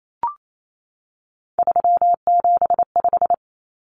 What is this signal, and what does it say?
Morse code, heard as a keyed tone, about 28 wpm, 700 Hz. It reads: 375
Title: Common Cut Numbers - 3-Digit 28wpm